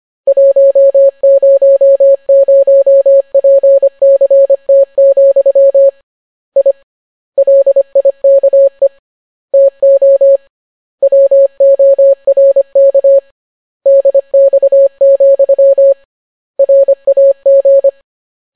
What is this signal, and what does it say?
Morse code, heard as a keyed tone, about 25 wpm, 550 Hz. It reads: 100PCT, I LIKE TO WORK DX, RAG